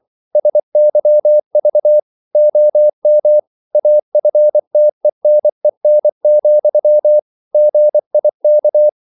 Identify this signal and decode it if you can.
Morse code, heard as a keyed tone, about 24 wpm, 600 Hz. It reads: SYV OM AFTENEN, GIK
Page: Ferie 24 wpm